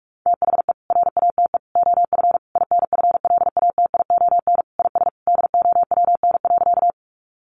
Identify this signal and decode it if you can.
Morse code, heard as a keyed tone, about 44 wpm, 700 Hz. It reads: THE RATE OF INFLATION IS DOWN.
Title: Sentences from Top 1100 Words 45wpm